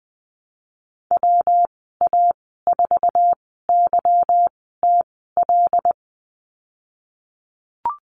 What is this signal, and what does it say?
Morse code, heard as a keyed tone, about 20 wpm, spaced as for 15 wpm, 700 Hz. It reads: WA4YTL
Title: 147 Lesson 24g - Letter y - Call Signs 20wpm